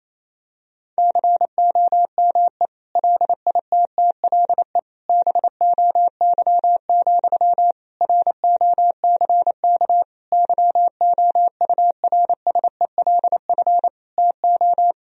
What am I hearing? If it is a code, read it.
Morse code, heard as a keyed tone, about 28 wpm, 700 Hz. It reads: COME LITTLE BOY, ROCK YOURSELF TO
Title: Sentences from Top 1000 Words - Review Part 2 - 28wpm